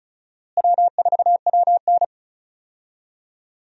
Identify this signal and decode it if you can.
Morse code, heard as a keyed tone, about 34 wpm, 700 Hz. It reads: W4WN
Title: USA Call Signs - 4 Characters - Repeated 2x - 35wpm